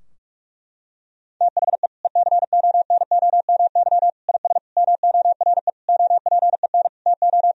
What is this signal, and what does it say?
Morse code, heard as a keyed tone, about 46 wpm, 700 Hz. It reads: THE ECONOMY IS MORE OPEN TO
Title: Sentences from Top 1000 Words - Review Part 3 - 45wpm